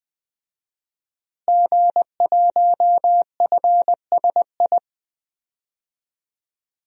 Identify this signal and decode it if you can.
Morse code, heard as a keyed tone, about 20 wpm, 700 Hz. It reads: G1FSI
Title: DXCC-100 Call Signs - 5 Characters - Repeated 2x - 20wpm